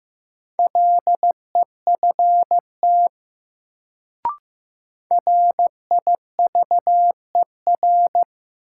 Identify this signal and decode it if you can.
Morse code, heard as a keyed tone, about 15 wpm, 700 Hz. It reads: LEFT RIVER
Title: Sentences from Top 300 Words - New Word Drill - 15wpm